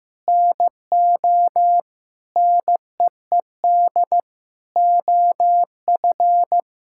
Morse code, heard as a keyed tone, about 15 wpm, 700 Hz. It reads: NO NEED OF
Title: Sentences from Top 1000 Words - Review Part 5 - 15wpm